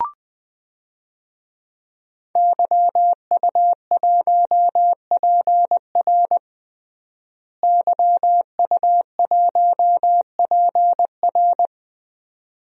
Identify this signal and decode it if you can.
Morse code, heard as a keyed tone, about 21 wpm, 700 Hz. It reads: YU1PR YU1PR
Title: DXCC-100 Call Signs - 5 Characters - Repeated 2x - 20wpm